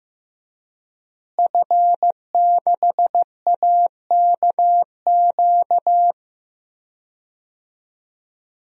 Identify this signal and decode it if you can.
Morse code, heard as a keyed tone, about 15 wpm, 700 Hz. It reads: F6AKQ
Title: DXCC-100 Call Signs - 5 Characters - Repeated 3x - 15wpm